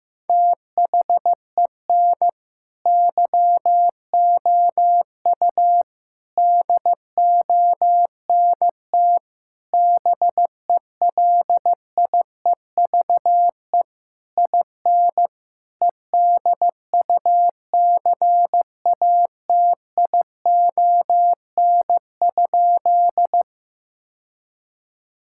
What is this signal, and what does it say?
Morse code, heard as a keyed tone, about 15 wpm, 700 Hz. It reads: THEN YOU DONT BELIEVE IN EDUCATION?
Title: Sentences from Top 1000 Words - Review Part 6 - 15wpm